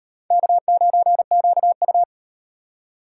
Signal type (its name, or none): Morse code